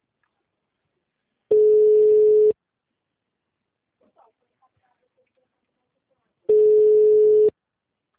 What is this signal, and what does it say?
Ringback tone (Europe, 425 Hz).
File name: out- -2017- - - .95369